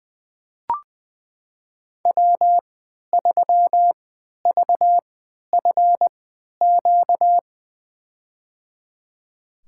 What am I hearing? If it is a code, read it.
Morse code, heard as a keyed tone, about 20 wpm, spaced as for 11 wpm, 700 Hz. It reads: W3VFQ
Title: USA Call Signs - 5 Characters - 3x Character Spacing - 20wpm